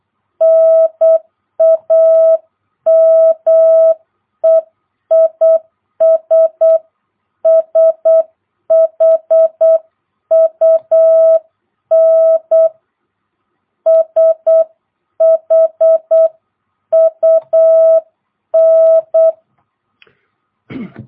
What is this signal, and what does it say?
Morse code, heard as a keyed tone, about 8 wpm, 660 Hz. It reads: NAME IS SHUN SHUN
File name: jh8cbh / 2022cw